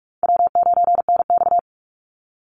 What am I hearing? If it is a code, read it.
Morse code, heard as a keyed tone, about 45 wpm, 700 Hz. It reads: W9NX
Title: Easy Call Signs - 45wpm